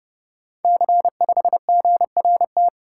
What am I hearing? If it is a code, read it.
Morse code, heard as a keyed tone, about 30 wpm, 700 Hz. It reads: C5GRT
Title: Code Groups - 5 Letter:Numbers - 30wpm